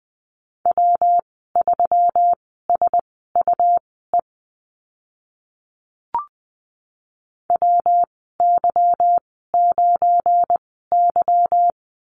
Morse code, heard as a keyed tone, about 20 wpm, spaced as for 14 wpm, 700 Hz. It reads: W3SUE WY9Y
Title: 179 Lesson 29g - Slash - Call Signs 20wpm